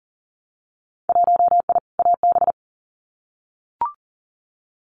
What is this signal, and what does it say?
Morse code, heard as a keyed tone, about 40 wpm, 700 Hz. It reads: 1I AB
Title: Winter Field Day Exchanges Encore 40wpm